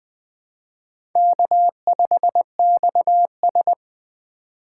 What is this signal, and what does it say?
Morse code, heard as a keyed tone, about 20 wpm, 700 Hz. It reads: K5XS
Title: USA Call Signs - 4 Characters - 4x Character Spacing - 20wpm